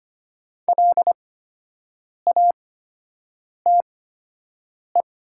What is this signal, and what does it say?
Morse code, heard as a keyed tone, about 25 wpm, spaced as for 7 wpm, 700 Hz. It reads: LATE
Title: 8x Character Spacing - Top 300 Words - 25wpm